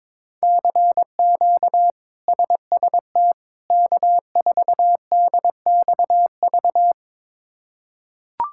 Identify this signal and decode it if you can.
Morse code, heard as a keyed tone, about 21 wpm, 700 Hz. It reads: CQ SST K4DXV
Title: K1USN SST QSO Practice - 22wpm